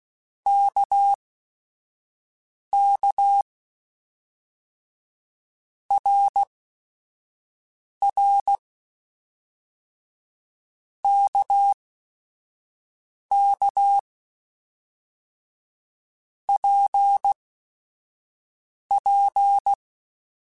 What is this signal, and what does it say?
Morse code, heard as a keyed tone, about 16 wpm, spaced as for 5 wpm, 800 Hz. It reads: KK RR KK PP
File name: Basic Sets / Basic Letter Groups Collective7 / RKPX@5